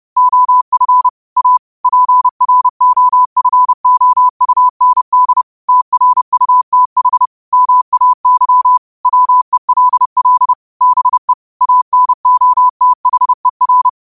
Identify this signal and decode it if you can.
Morse code, heard as a keyed tone, about 30 wpm, 1000 Hz. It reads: OF A PROFOUND TRUTH MAY WELL BE ANOTHER